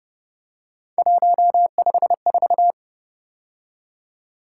Morse code, heard as a keyed tone, about 31 wpm, 700 Hz. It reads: 154